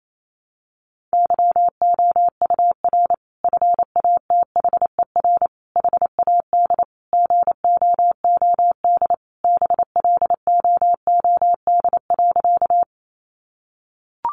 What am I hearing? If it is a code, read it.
Morse code, heard as a keyed tone, about 28 wpm, 700 Hz. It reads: YOUR FATHER HAD GOOD BLOOD.
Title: Sentences from Top 1100 Words 28wpm